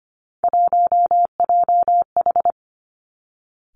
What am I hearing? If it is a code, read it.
Morse code, heard as a keyed tone, about 25 wpm, 700 Hz. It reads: 1JH
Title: Three Challenging Letter-Number Combinations 25wpm